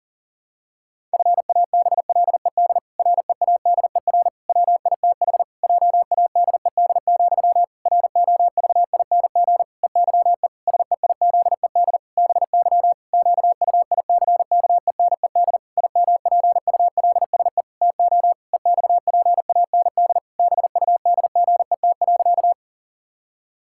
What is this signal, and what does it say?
Morse code, heard as a keyed tone, about 40 wpm, 700 Hz. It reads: FABLED READER WITH JADED, ROVING EYE SEIZED BY QUICKENED IMPULSE TO EXPAND BUDGET.
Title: Pangram Challenge v2 - 40wpm